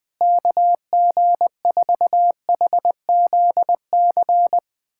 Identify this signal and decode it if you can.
Morse code, heard as a keyed tone, about 20 wpm, 700 Hz. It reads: KG4HZC
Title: USA Callsign DE Callsign - 6 Characters - 20wpm